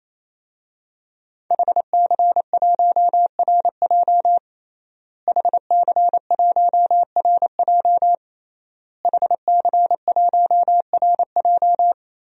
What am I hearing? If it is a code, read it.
Morse code, heard as a keyed tone, about 28 wpm, 700 Hz. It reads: HC1RJ HC1RJ HC1RJ
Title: DXCC-100 Call Signs - 5 Characters - Repeated 3x - 28wpm